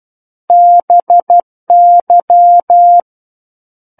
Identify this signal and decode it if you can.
Morse code, heard as a keyed tone, about 12 wpm, 700 Hz. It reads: BY
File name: Book One / 03-Chapter 3 - 12 WPM